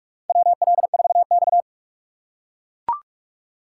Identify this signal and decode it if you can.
Morse code, heard as a keyed tone, about 42 wpm, 700 Hz. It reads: WL4X
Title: USA Call Signs - 4 Characters - 6x Character Spacing - 45wpm